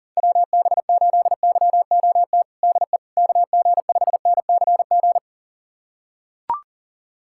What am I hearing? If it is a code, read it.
Morse code, heard as a keyed tone, about 40 wpm, 700 Hz. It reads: WB8YOT DE KG5NCG
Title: USA Callsign DE Callsign 40wpm